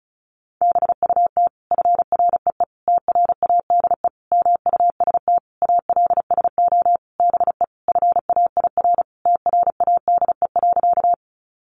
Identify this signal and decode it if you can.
Morse code, heard as a keyed tone, about 34 wpm, 700 Hz. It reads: BUT FREE TRADE MUST ALSO BE FAIR TRADE.
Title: Sentences from Top 1300 Words 35wpm